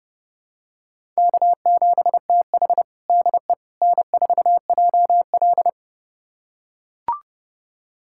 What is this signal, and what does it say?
Morse code, heard as a keyed tone, about 30 wpm, 700 Hz. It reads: K7TH DE N4JL
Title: USA Callsign DE Callsign - 4 Characters - 30wpm